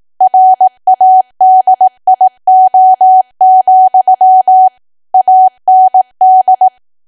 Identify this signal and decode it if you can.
Morse code, heard as a keyed tone, about 18 wpm, 750 Hz. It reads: RADIO, AND
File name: Archive / 18 WPM / 18WPM